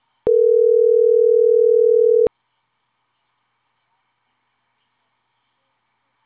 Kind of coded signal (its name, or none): telephone call progress tone